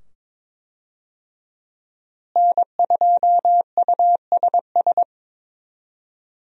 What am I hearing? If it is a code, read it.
Morse code, heard as a keyed tone, about 22 wpm, 700 Hz. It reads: N2USS
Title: USA Call Signs - 5 Characters - Repeated 2x - 22wpm